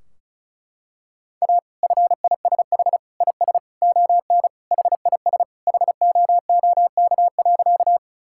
Morse code, heard as a keyed tone, about 36 wpm, 700 Hz. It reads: A FISH IS ON HIS HOOK.
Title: Sentences from Top 1600 Words 35wpm